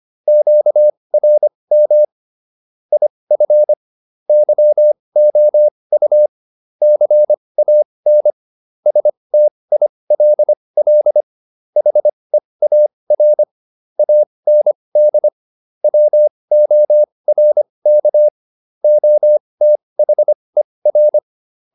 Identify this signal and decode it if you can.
Morse code, heard as a keyed tone, about 25 wpm, spaced as for 20 wpm, 600 Hz. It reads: QRM IF YOU CAN STILL HEAR AND WORK OTHER